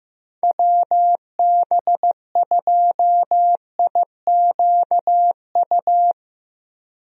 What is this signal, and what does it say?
Morse code, heard as a keyed tone, about 16 wpm, 700 Hz. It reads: WB2IQU
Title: USA Call Signs - 6 Characters - Repeated 3x - 15wpm